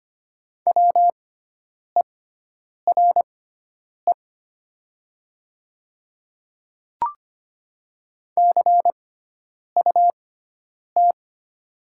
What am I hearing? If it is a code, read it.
Morse code, heard as a keyed tone, about 25 wpm, spaced as for 9 wpm, 700 Hz. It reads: WERE CUT
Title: 6x Character Spacing - Top 500 Words - 25wpm